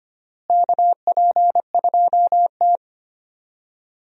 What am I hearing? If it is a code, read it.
Morse code, heard as a keyed tone, about 25 wpm, 700 Hz. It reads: KP2T
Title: DXCC-100 Call Signs - 4 Characters - Repeated 3x - 25wpm